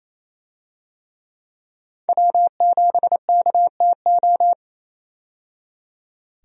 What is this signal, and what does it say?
Morse code, heard as a keyed tone, about 28 wpm, 700 Hz. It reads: W7KTO
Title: USA Call Signs - 5 Characters - 28wpm